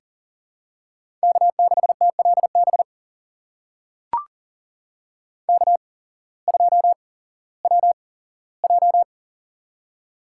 Morse code, heard as a keyed tone, about 40 wpm, spaced as for 19 wpm, 700 Hz. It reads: K6TLB K2WJ
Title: USA Call Signs - 8x Character Spacing - 40wpm